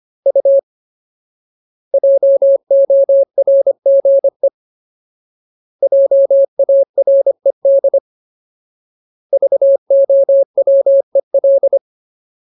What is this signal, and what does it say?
Morse code, heard as a keyed tone, about 25 wpm, 550 Hz. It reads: U JORGE JARED VOWEL
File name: Ultimate Word List / 25WPM Ultimate